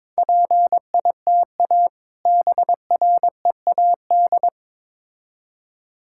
Morse code, heard as a keyed tone, about 22 wpm, 700 Hz. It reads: PITA BREAD
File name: Difficult Words with Hints - 22wpm